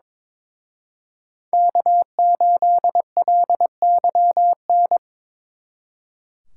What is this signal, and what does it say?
Morse code, heard as a keyed tone, about 26 wpm, 700 Hz. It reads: K8LYN
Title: Speed Racing - USA 1x3 Call Signs 22wpm